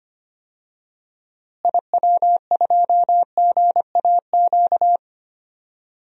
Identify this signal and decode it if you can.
Morse code, heard as a keyed tone, about 25 wpm, 700 Hz. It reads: IW2GAQ
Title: DXCC-100 Call Signs - 6 Characters - Repeated 3x - 25wpm